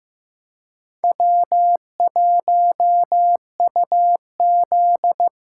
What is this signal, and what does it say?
Morse code, heard as a keyed tone, about 16 wpm, 700 Hz. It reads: W1UZ
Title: Speed Racing - USA Call Signs 15wpm